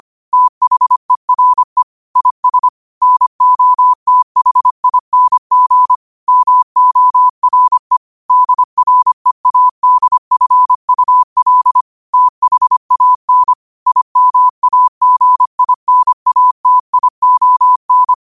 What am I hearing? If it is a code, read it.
Morse code, heard as a keyed tone, about 25 wpm, 1000 Hz. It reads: THERE IS NOTHING MORE DREADFUL THAN IMAGINATION
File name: qotdfiles / 25wpm